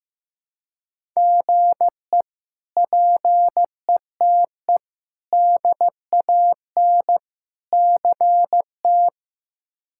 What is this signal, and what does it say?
Morse code, heard as a keyed tone, about 15 wpm, 700 Hz. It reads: GE PETE DAN CT
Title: K1USN SST QSO Practice - 15wpm